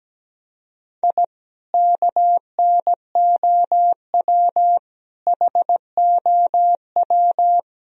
Morse code, heard as a keyed tone, about 17 wpm, 700 Hz. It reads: I KNOW HOW
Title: Sentences from Top 1000 Words - Review Part 1 - 17wpm